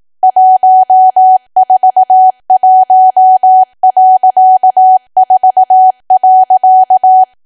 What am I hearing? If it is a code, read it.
Morse code, heard as a keyed tone, about 18 wpm, 750 Hz. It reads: 141.4.
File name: cwa / W1AW-Level2 / 18WPM